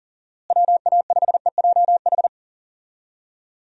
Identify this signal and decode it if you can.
Morse code, heard as a keyed tone, about 40 wpm, 700 Hz. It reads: WA5EJH
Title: USA Call Signs - 6 Characters - Repeated 2x - 40wpm